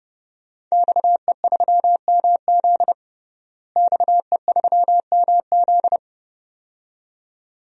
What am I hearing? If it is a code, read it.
Morse code, heard as a keyed tone, about 30 wpm, 700 Hz. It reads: XE3MZ XE3MZ
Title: DXCC-100 Call Signs - 5 Characters - Repeated 2x - 30wpm